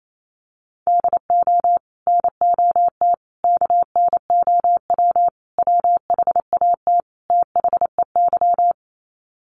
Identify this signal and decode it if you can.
Morse code, heard as a keyed tone, about 28 wpm, 700 Hz. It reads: DO NOT KNOW WHAT THEY
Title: Sets of 5 Words 28wpm